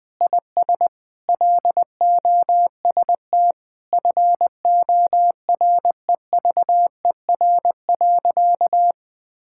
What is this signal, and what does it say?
Morse code, heard as a keyed tone, about 20 wpm, 700 Hz. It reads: IS LOST FOREVER.
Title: Sentences from Top 1600 Words 20wpm